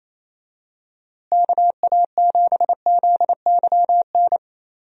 Morse code, heard as a keyed tone, about 28 wpm, 700 Hz. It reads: KA7ZYN
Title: USA Call Signs - 6 Characters - Repeated 2x - 28wpm